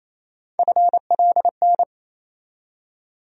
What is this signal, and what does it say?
Morse code, heard as a keyed tone, about 28 wpm, 700 Hz. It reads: FLN
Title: USA Call Sign Postfixes - 3 Letters 28wpm